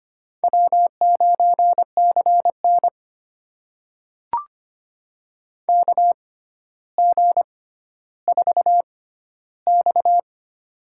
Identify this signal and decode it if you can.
Morse code, heard as a keyed tone, about 25 wpm, spaced as for 13 wpm, 700 Hz. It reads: W9CN KG4X
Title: USA Call Signs - 4 Characters - 6x Character Spacing - 25wpm